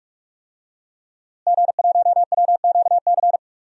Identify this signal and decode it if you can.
Morse code, heard as a keyed tone, about 46 wpm, 700 Hz. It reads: G1WQC
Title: DXCC-100 Call Signs - 5 Characters - Repeated 2x - 45wpm